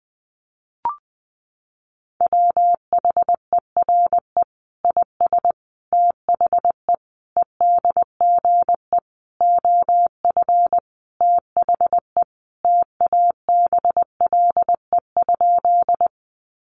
Morse code, heard as a keyed tone, about 20 wpm, 700 Hz. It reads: WHERE IS THE EDGE OF THE TABLE?
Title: Sentences from Top 1300 Words 20wpm